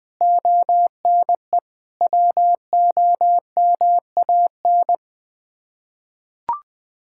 Morse code, heard as a keyed tone, about 20 wpm, 700 Hz. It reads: ONE WOMAN
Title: Sets of 6 Words - 20wpm